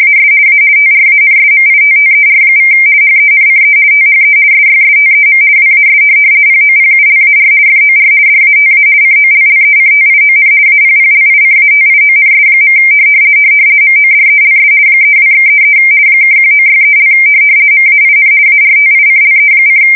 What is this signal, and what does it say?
RTTY, about 50 baud. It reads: AS A TRANSMITTER= HENCE THE SYSTEM WAS POPULAR FOR DATA COLLECTION APPLICATIONS. MODELS 32/33 (1963) MAIN ARTICLE:
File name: fsk170-50B / teletype-corp